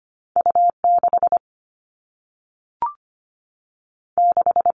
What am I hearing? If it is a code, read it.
Morse code, heard as a keyed tone, about 25 wpm, 700 Hz. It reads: U6 6
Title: All Cut Numbers - 2-Digit 25wpm